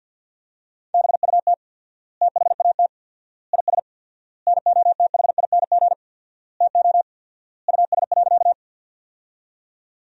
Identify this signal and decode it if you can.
Morse code, heard as a keyed tone, about 46 wpm, 700 Hz. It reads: BUT THAT IS NOTHING TO US.
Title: 4x Word-Spacing - Sentences from Top 500 Words - 50wpm